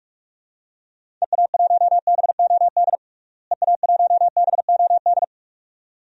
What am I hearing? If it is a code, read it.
Morse code, heard as a keyed tone, about 46 wpm, 700 Hz. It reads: EA1BOD EA1BOD
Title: DXCC-100 Call Signs - 6 Characters - Repeated 2x - 45wpm